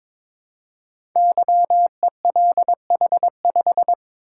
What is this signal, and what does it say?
Morse code, heard as a keyed tone, about 22 wpm, 700 Hz. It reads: YELH5
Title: Code Groups - 5 Letter:Numbers - 22wpm